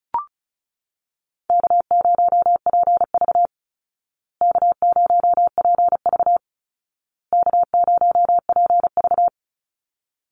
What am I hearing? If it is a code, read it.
Morse code, heard as a keyed tone, about 34 wpm, 700 Hz. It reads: K0PV K0PV K0PV
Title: Call Signs Repeated 2x - 35wpm